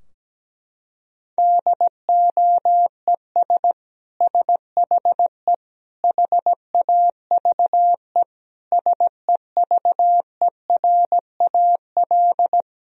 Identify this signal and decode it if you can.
Morse code, heard as a keyed tone, about 17 wpm, 700 Hz. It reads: DOES SHE HAVE SEVERAL
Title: Sentences from Top 1000 Words - Review Part 2 - 17wpm